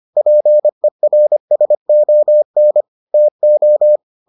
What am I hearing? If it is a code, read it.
Morse code, heard as a keyed tone, about 25 wpm, 600 Hz. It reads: PERSON TO